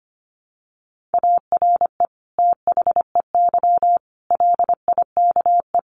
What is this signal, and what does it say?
Morse code, heard as a keyed tone, about 25 wpm, 700 Hz. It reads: ARE THEY LIKE
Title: Sentences from Top 200 Words 25wpm